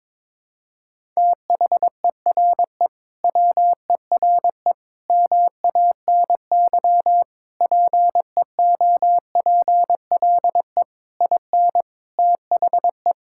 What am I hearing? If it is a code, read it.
Morse code, heard as a keyed tone, about 22 wpm, 700 Hz. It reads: THERE WERE MANY PEOPLE IN THE
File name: practice / 22wpm / Sentences with 3-Word Phrases - 22wpm